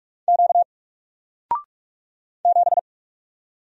Morse code, heard as a keyed tone, about 44 wpm, 700 Hz. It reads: Q 7
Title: Single Character (Letter:Number:Punctuation:Prosign) 45wpm